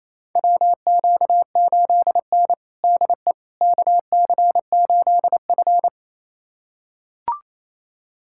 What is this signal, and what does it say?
Morse code, heard as a keyed tone, about 28 wpm, 700 Hz. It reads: WQ8N DE KC8F
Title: USA Callsign DE Callsign - 4 Characters - 28wpm